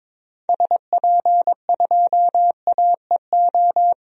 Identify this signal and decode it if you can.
Morse code, heard as a keyed tone, about 22 wpm, 700 Hz. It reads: SP2AEO
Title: DXCC-100 Call Signs - 6 Characters - 22wpm